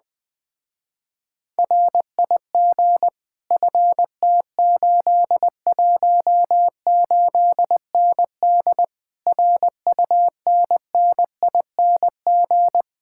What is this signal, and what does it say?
Morse code, heard as a keyed tone, about 20 wpm, 700 Hz. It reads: RIG FT818ND RUNNING
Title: Speed-Racing Common QSO Phrases - 17wpm